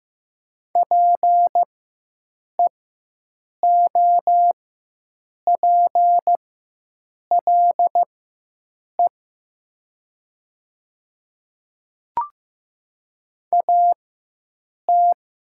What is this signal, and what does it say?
Morse code, heard as a keyed tone, about 15 wpm, spaced as for 7 wpm, 700 Hz. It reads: PEOPLE AT
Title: 4x Character Spacing - Top 100 Words - 15wpm